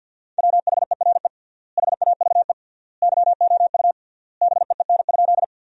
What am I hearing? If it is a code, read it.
Morse code, heard as a keyed tone, about 50 wpm, 700 Hz. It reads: WHERE HAVE YOU BEEN?
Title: 3x Word-Spacing - Sentences from Top 200 Words - 50wpm